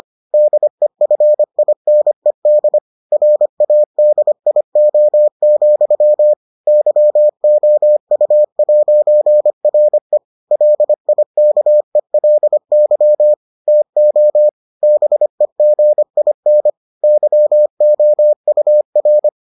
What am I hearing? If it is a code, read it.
Morse code, heard as a keyed tone, about 25 wpm, 600 Hz. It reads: DEFINED RADIO, YOU'RE LIKELY TO BEGIN YOUR